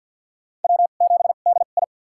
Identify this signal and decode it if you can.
Morse code, heard as a keyed tone, about 50 wpm, spaced as for 36 wpm, 700 Hz. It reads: W7DI
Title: USA Call Signs - 4 Characters - 2x Character Spacing - 50wpm